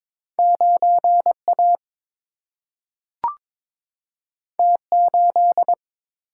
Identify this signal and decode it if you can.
Morse code, heard as a keyed tone, about 22 wpm, 700 Hz. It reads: 9 A T 8
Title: Single-Letter Single-Number 22wpm